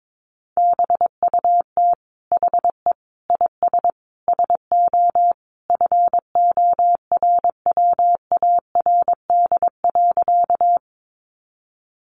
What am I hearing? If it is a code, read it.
Morse code, heard as a keyed tone, about 22 wpm, 700 Hz. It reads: BUT HE IS SO FORWARD.
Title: Sentences from Top 1300 Words 22wpm